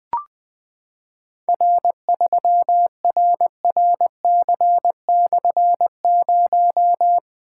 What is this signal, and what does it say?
Morse code, heard as a keyed tone, about 21 wpm, 700 Hz. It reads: R3RRC/0
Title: Difficult Call Signs 20wpm